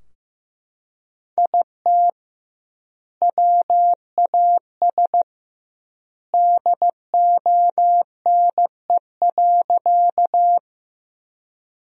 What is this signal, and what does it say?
Morse code, heard as a keyed tone, about 15 wpm, 700 Hz. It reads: IT WAS DONE.
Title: 2x Word-Spacing - Sentences from Top 500 Words - 15wpm